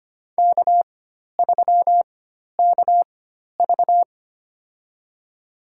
Morse code, heard as a keyed tone, about 25 wpm, spaced as for 12 wpm, 700 Hz. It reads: K3KV
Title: USA Call Signs - 4 Characters - 4x Character Spacing - 25wpm